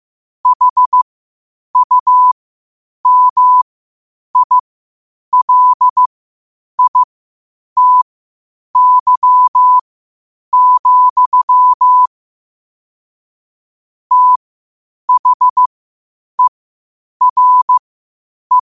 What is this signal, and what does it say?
Morse code, heard as a keyed tone, about 15 wpm, spaced as for 8 wpm, 1000 Hz. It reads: HUMILITY, THERE
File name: qotdfiles / 8wpm